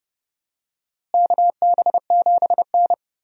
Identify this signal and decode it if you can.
Morse code, heard as a keyed tone, about 30 wpm, 700 Hz. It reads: KB7N
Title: USA Call Signs - 4 Characters - Repeated 3x - 30wpm